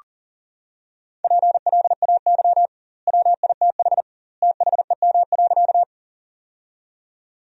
Morse code, heard as a keyed tone, about 40 wpm, 700 Hz. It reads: PLAY WITH THEM.
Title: 2x Word-Spacing - Sentences from Top 200 Words - 40wpm